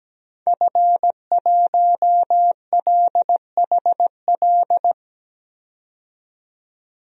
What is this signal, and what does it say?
Morse code, heard as a keyed tone, about 17 wpm, 700 Hz. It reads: F1LHL
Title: DXCC-100 Call Signs - 5 Characters - 17wpm